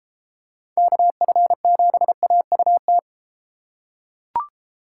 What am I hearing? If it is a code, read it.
Morse code, heard as a keyed tone, about 31 wpm, 700 Hz. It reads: KF7AUT